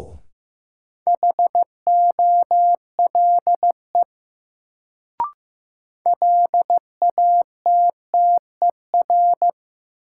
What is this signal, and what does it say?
Morse code, heard as a keyed tone, about 15 wpm, 700 Hz. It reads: HOLE LATTER
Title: Sentences from Top 1600 Words - New Word Drill Alt - 15wpm